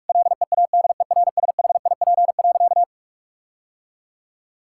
Morse code, heard as a keyed tone, about 46 wpm, 700 Hz. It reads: LEADERSHIP.
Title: Sentences from Top 1600 Words 45wpm